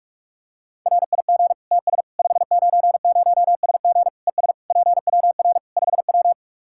Morse code, heard as a keyed tone, about 41 wpm, 700 Hz. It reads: RIG TS 590SG ES PWR 5W
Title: Practice QSO - Exchange 2 - Info - 45wpm